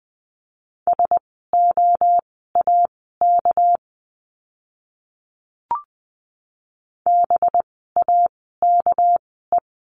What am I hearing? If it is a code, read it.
Morse code, heard as a keyed tone, about 20 wpm, spaced as for 14 wpm, 700 Hz. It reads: SOAK BAKE
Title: 199 Lesson 32f - Letter k - Words 20wpm